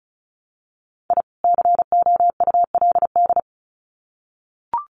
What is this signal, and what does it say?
Morse code, heard as a keyed tone, about 34 wpm, 700 Hz. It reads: I COULD
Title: Sets of 2 Words 35wpm